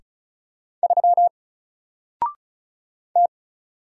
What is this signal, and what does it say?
Morse code, heard as a keyed tone, about 34 wpm, 700 Hz. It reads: 3 T
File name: Single-Letter-Number-35wpm